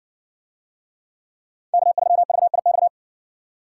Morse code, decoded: K3VIX

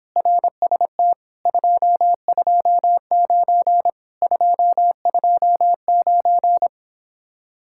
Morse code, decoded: RST 229 229

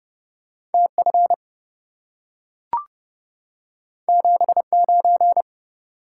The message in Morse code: TF 79

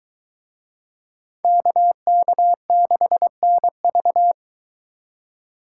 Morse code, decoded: KK6NV

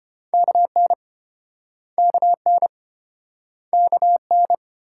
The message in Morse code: KN KN KN

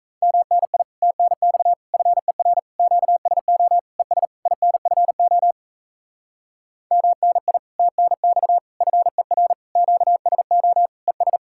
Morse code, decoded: MNI TNX FER QSO ES INFO MNI TNX FER QSO ES

